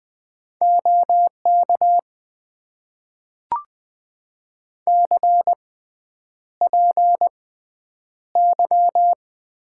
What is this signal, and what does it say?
Morse code, heard as a keyed tone, about 20 wpm, spaced as for 9 wpm, 700 Hz. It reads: OK CPY
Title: Top 90 QSO Elements - 6x Character Spacing - 20wpm